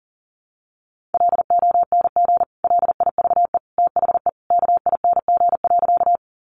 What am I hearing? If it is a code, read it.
Morse code, heard as a keyed tone, about 40 wpm, 700 Hz. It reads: LONG LIVE THE KING.
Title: Sentences from Top 400 Words 40wpm